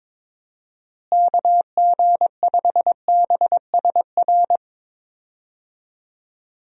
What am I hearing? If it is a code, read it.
Morse code, heard as a keyed tone, about 21 wpm, 700 Hz. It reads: KG5BSR
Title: USA Call Signs - 6 Characters - Repeated 3x - 22wpm